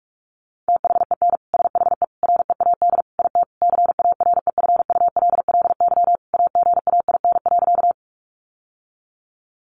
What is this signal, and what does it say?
Morse code, heard as a keyed tone, about 46 wpm, 700 Hz. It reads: THEN SHE READ IT CAREFULLY AGAIN.
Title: Sentences from Top 1300 Words 45wpm